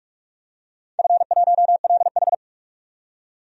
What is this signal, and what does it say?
Morse code, heard as a keyed tone, about 42 wpm, 700 Hz. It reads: F1LH
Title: DXCC-100 Call Signs - 4 Characters - 45wpm